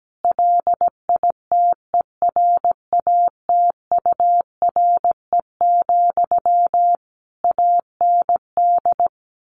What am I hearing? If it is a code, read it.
Morse code, heard as a keyed tone, about 17 wpm, 700 Hz. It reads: LITERATURE, AND
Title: Sentences from Top 600 Words 17wpm